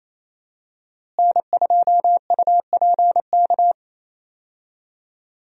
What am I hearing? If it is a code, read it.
Morse code, heard as a keyed tone, about 28 wpm, 700 Hz. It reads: N2UPK
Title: USA Call Signs - 5 Characters - Repeated 3x - 28wpm